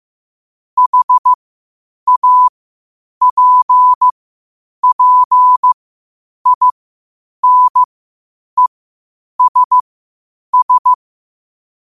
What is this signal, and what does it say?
Morse code, heard as a keyed tone, about 15 wpm, spaced as for 8 wpm, 1000 Hz. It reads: HAPPINESS
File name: qotdfiles / 8wpm